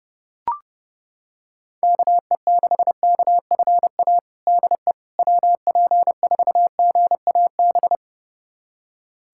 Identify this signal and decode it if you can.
Morse code, heard as a keyed tone, about 32 wpm, 700 Hz. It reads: KE6KFA DE WP4GAB
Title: USA Callsign DE Callsign - 6 Characters - 30wpm